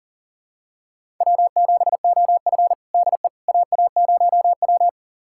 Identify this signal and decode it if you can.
Morse code, heard as a keyed tone, about 40 wpm, 700 Hz. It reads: W7OF DE AA0W
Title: USA Callsign DE Callsign - 4 Characters - 40wpm